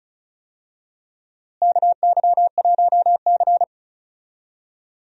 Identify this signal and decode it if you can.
Morse code, heard as a keyed tone, about 34 wpm, 700 Hz. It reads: KY1C
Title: USA Call Signs - 4 Characters - Repeated 2x - 35wpm